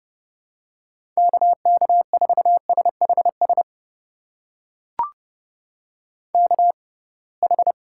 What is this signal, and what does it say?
Morse code, heard as a keyed tone, about 30 wpm, spaced as for 23 wpm, 700 Hz. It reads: KK4SHS KH
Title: USA Call Signs - 6 Characters - 6x Character Spacing - 30wpm